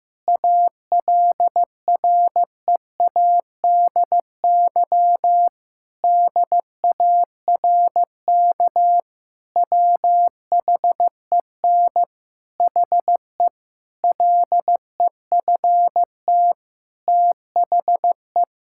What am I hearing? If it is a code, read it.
Morse code, heard as a keyed tone, about 15 wpm, 700 Hz. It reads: ALREADY DARK WHEN HE LEFT THE
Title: Sentences from Top 1000 Words - Review Part 5 - 15wpm